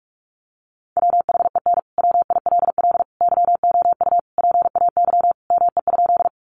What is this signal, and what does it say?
Morse code, heard as a keyed tone, about 44 wpm, 700 Hz. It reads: WHEN WILL YOU PAY ME?
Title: Sentences from Top 700 Words 45wpm